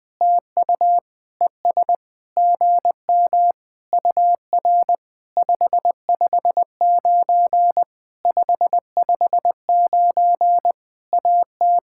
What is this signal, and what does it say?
Morse code, heard as a keyed tone, about 20 wpm, 700 Hz. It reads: TU ES GM UR 559 559 AT
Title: POTA QSO Practice - 20wpm